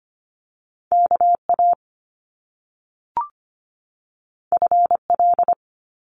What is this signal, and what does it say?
Morse code, heard as a keyed tone, about 25 wpm, 700 Hz. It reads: KA FL
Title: Top Two Letter Combinations in Words 25wpm